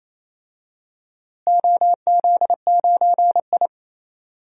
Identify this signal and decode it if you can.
Morse code, heard as a keyed tone, about 28 wpm, 700 Hz. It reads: OZ9I
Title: DXCC-100 Call Signs - 4 Characters - Repeated 2x - 28wpm